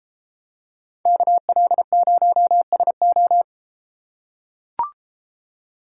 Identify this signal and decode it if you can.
Morse code, heard as a keyed tone, about 33 wpm, 700 Hz. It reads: KL0SO